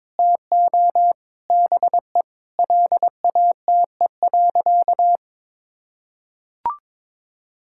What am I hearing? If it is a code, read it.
Morse code, heard as a keyed tone, about 21 wpm, 700 Hz. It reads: TO BE LATE.
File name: practice / 22wpm / Sentences with 4-Word Phrases - 22wpm